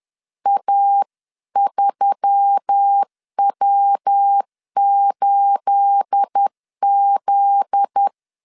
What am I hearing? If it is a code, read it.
Morse code, heard as a keyed tone, about 10 wpm, 800 Hz. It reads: A3W8Z
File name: CW / DET / 1 9 1500C S A 10wpm 4511